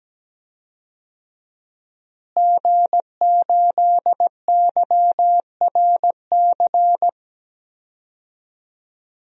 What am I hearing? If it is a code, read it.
Morse code, heard as a keyed tone, about 17 wpm, 700 Hz. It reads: G8YRC
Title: DXCC-100 Call Signs - 5 Characters - Repeated 2x - 17wpm